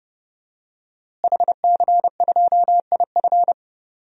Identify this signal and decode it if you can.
Morse code, heard as a keyed tone, about 30 wpm, 700 Hz. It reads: HC2IF
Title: DXCC-100 Call Signs - 5 Characters - Repeated 2x - 30wpm